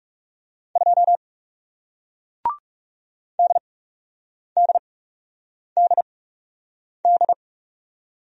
Morse code, heard as a keyed tone, about 40 wpm, spaced as for 8 wpm, 700 Hz. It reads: 2 DDDD